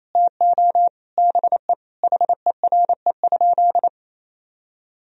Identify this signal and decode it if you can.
Morse code, heard as a keyed tone, about 27 wpm, 700 Hz. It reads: TO BE HERE?
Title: Sentences from Top 1000 Words - Review Part 4 - 28wpm